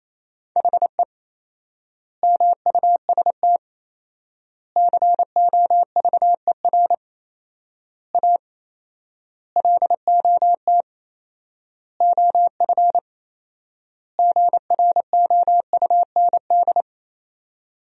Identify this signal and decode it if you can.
Morse code, heard as a keyed tone, about 28 wpm, 700 Hz. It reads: HE MUST COVER A LOT OF GROUND
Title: 4x Word-Spacing - Sentences from Top 500 Words - 28wpm